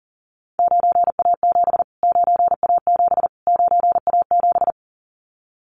Morse code, heard as a keyed tone, about 40 wpm, 700 Hz. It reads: 9A7 9A7 9A7
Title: Callsign Prefixes for Top 25 DXCC - Repeated 3x - 40wpm